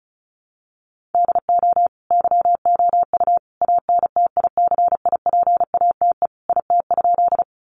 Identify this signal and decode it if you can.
Morse code, heard as a keyed tone, about 34 wpm, 700 Hz. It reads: DO YOU ANTICIPATE IT?
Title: Sentences from Top 1500 Words 35wpm